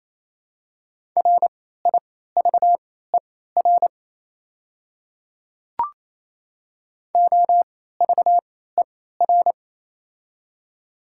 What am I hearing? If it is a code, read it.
Morse code, heard as a keyed tone, about 28 wpm, spaced as for 16 wpm, 700 Hz. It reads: RIVER OVER
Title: 3x Character Spacing - Top 500 Words - 28wpm